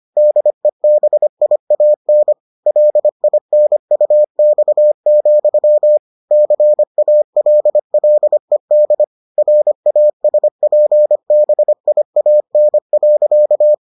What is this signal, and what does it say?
Morse code, heard as a keyed tone, about 25 wpm, 600 Hz. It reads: DEBIAN LINUX, CALLED RASPBIAN.